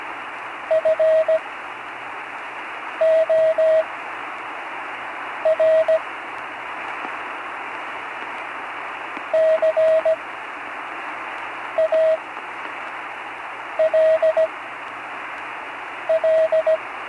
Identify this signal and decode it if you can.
Morse code, heard as a keyed tone, about 17 wpm, spaced as for 5 wpm, 640 Hz, with radio noise. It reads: FOR CALL
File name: CW Learning Programs / Learning Morse Code (from Tune in the World) / 43 on the air